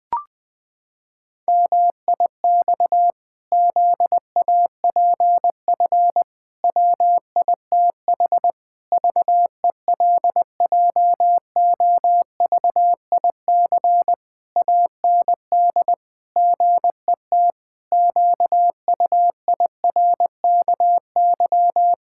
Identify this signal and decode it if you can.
Morse code, heard as a keyed tone, about 20 wpm, 700 Hz. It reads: MIX ZAPF WITH VELJOVIC AND GET QUIRKY
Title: Pangram Challenge v2 - 20wpm